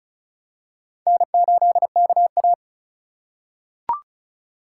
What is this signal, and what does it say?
Morse code, heard as a keyed tone, about 34 wpm, 700 Hz. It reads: N8KA
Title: USA Call Signs - 4 Characters - 3x Character Spacing - 35wpm